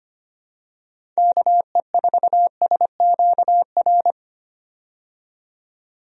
Morse code, heard as a keyed tone, about 25 wpm, 700 Hz. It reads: KE4SQR
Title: USA Call Signs - 6 Characters - 25wpm